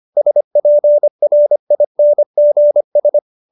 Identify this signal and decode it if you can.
Morse code, heard as a keyed tone, about 25 wpm, 590 Hz. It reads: SPRINGS
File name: QSO312 25